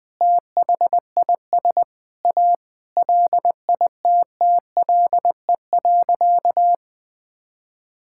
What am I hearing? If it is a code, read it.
Morse code, heard as a keyed tone, about 20 wpm, 700 Hz. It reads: THIS A LITTLE.
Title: Sentences from Top 1600 Words 20wpm